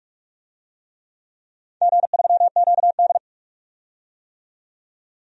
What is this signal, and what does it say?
Morse code, heard as a keyed tone, about 44 wpm, 700 Hz. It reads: G3QD
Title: DXCC-100 Call Signs - 4 Characters - 45wpm